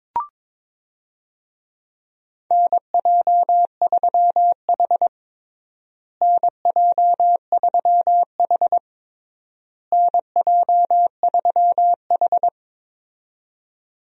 Morse code, heard as a keyed tone, about 22 wpm, 700 Hz. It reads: NJ3H NJ3H NJ3H
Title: USA Call Signs - 4 Characters - Repeated 3x - 22wpm